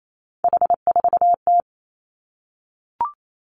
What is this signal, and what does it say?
Morse code, heard as a keyed tone, about 28 wpm, 700 Hz. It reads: H4T
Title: Three Challenging Letter-Number Combinations 28wpm